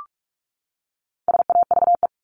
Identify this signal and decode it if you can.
Morse code, heard as a keyed tone, about 44 wpm, 700 Hz. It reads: SAVE
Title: Top 1001-1250 Words 45wpm